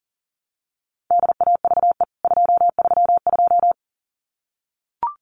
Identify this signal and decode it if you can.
Morse code, heard as a keyed tone, about 40 wpm, 700 Hz. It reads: DAVE 232
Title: CWT Exchange 40wpm